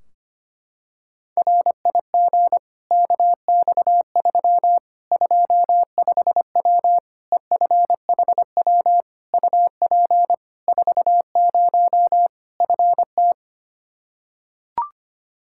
Morse code, heard as a keyed tone, about 25 wpm, 700 Hz. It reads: RIG KX3 25W EFHW UP 40 FT
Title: Practice QSO - Exchange 2 - Info - 25wpm